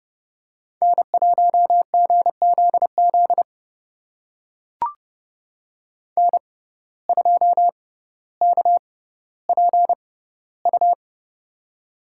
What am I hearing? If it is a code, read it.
Morse code, heard as a keyed tone, about 31 wpm, spaced as for 16 wpm, 700 Hz. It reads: N1GZZ N2KPU
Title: USA Call Signs - 5 Characters - 6x Character Spacing - 30wpm